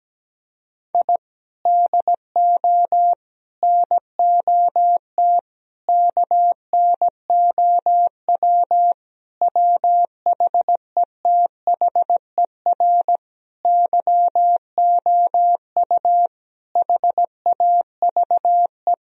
Morse code, heard as a keyed tone, about 17 wpm, 700 Hz. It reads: I DO NOT KNOW WHETHER YOU HAVE
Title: Sentences from Top 1000 Words - Review Part 3 - 17wpm